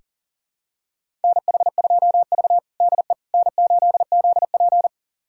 Morse code, heard as a keyed tone, about 40 wpm, 700 Hz. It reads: NH2V DE N8ZP
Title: USA Callsign DE Callsign - 4 Characters - 40wpm